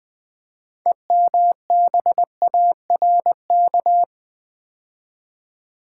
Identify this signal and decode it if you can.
Morse code, heard as a keyed tone, about 20 wpm, 700 Hz. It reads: EMBARK
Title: Build A Word - 3 Letters At A Time - 6-Letter Words - 20wpm